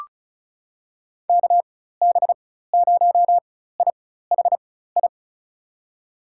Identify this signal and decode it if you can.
Morse code, heard as a keyed tone, about 34 wpm, spaced as for 16 wpm, 700 Hz. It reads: KB0IHI
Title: USA Call Signs - 6 Characters - 4x Character Spacing - 35wpm